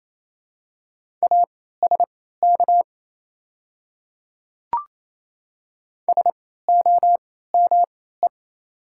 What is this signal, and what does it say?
Morse code, heard as a keyed tone, about 28 wpm, spaced as for 16 wpm, 700 Hz. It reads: ASK SOME
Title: 3x Character Spacing - Top 200 Words - 28wpm